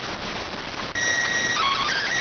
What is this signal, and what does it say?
SSTV (slow-scan television), PD 290.